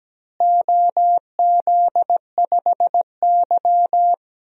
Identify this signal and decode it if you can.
Morse code, heard as a keyed tone, about 17 wpm, 700 Hz. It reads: OZ5Y
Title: DXCC-100 Call Signs - 4 Characters - Repeated 2x - 17wpm